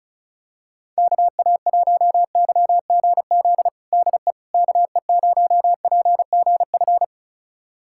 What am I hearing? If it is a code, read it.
Morse code, heard as a keyed tone, about 34 wpm, 700 Hz. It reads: KA1YGZ DE KE0PGF
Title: USA Callsign DE Callsign - 6 Characters - 35wpm